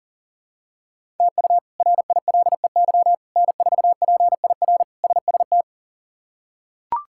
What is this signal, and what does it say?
Morse code, heard as a keyed tone, about 40 wpm, 700 Hz. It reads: TU RILEY N4PIR SST
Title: K1USN SST QSO Practice - 40wpm